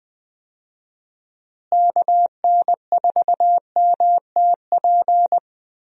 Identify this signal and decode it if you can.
Morse code, heard as a keyed tone, about 20 wpm, 700 Hz. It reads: KN4MTP
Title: USA Call Signs - 6 Characters - Repeated 3x - 20wpm